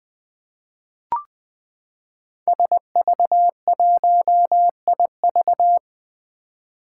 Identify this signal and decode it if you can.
Morse code, heard as a keyed tone, about 20 wpm, 700 Hz. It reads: SV1IV
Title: Five Challenging Letter-Number Combinations 20wpm